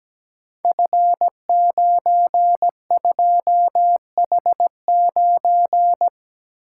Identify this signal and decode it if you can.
Morse code, heard as a keyed tone, about 17 wpm, 700 Hz. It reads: F92H9
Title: Code Groups - 5 Letter:Numbers - 17wpm